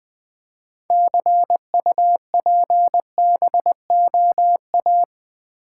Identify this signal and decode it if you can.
Morse code, heard as a keyed tone, about 20 wpm, 700 Hz. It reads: CUPBOA
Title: Build A Word - 3 Letters At A Time - 8-Letter Words - 20wpm